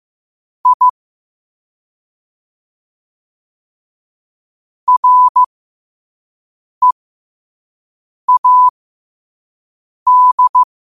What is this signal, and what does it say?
Morse code, heard as a keyed tone, about 15 wpm, spaced as for 5 wpm, 1000 Hz. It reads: I READ
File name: qotdfiles / 5wpm